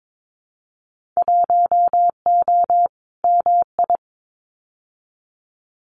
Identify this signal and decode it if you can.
Morse code, heard as a keyed tone, about 22 wpm, 700 Hz. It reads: 1O MI
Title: Winter Field Day Exchanges Encore 22wpm